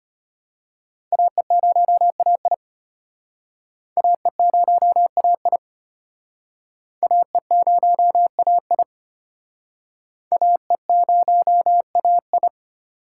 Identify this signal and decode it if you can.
Morse code, heard as a keyed tone, about 33 wpm, 700 Hz. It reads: AE0AI AE0AI AE0AI AE0AI